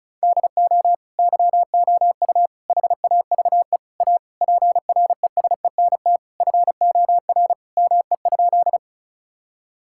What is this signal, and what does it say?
Morse code, heard as a keyed tone, about 36 wpm, 700 Hz. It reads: DO YOU HAVE A PRESENT FOR ME?
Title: Sentences from Top 1000 Words - Review Part 5 - 35wpm